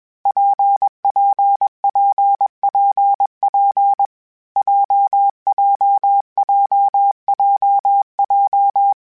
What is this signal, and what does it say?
Morse code, heard as a keyed tone, about 21 wpm, 800 Hz. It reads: PPPPP JJJJJ
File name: Difficult Combos / Difficult Combos-mp3s / WPJ WPJ1 @20